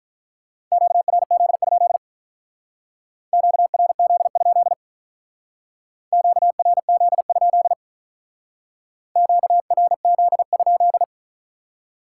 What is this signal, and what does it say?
Morse code, heard as a keyed tone, about 43 wpm, 700 Hz. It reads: QRZ? QRZ? QRZ? QRZ?
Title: Speed-Racing Top 180 QSO Elements 35wpm